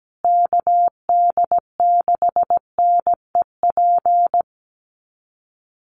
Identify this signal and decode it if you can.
Morse code, heard as a keyed tone, about 17 wpm, 700 Hz. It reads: KD6NEP
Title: USA Call Signs Repeated 3x 17wpm